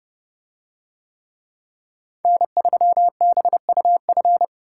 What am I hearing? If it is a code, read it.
Morse code, heard as a keyed tone, about 30 wpm, 700 Hz. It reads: N3BUF